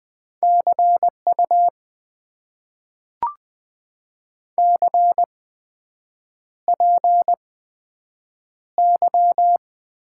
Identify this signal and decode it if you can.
Morse code, heard as a keyed tone, about 20 wpm, spaced as for 7 wpm, 700 Hz. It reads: CU CPY
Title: Top 90 QSO Elements - 8x Character Spacing - 20wpm